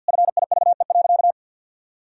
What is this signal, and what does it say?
Morse code, heard as a keyed tone, about 49 wpm, 700 Hz. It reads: FIVE.
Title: Sentences from Top 400 Words and Abbreviations 50wpm